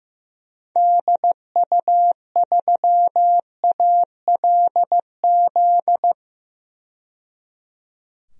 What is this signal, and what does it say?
Morse code, heard as a keyed tone, about 15 wpm, 700 Hz. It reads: DU3ALZ
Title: DXCC-100 Call Signs - 6 Characters - 15wpm